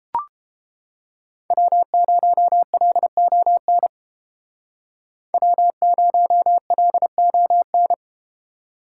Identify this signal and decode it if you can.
Morse code, heard as a keyed tone, about 29 wpm, 700 Hz. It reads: W0LON W0LON